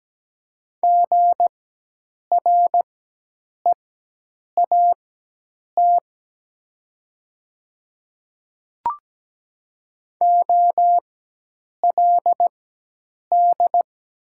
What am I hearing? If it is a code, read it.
Morse code, heard as a keyed tone, about 17 wpm, spaced as for 8 wpm, 700 Hz. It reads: GREAT OLD
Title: 4x Character Spacing - Top 200 Words - 17wpm